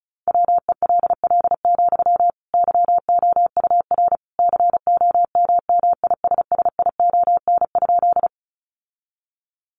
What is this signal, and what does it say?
Morse code, heard as a keyed tone, about 34 wpm, 700 Hz. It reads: WELL, YOUR COMMISSION?
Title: Sentences from Top 1400 Words 35wpm